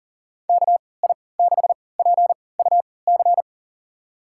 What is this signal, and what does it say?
Morse code, heard as a keyed tone, about 37 wpm, spaced as for 22 wpm, 700 Hz. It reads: KI6PUC
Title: USA Call Signs - 6 Characters - 3x Character Spacing - 40wpm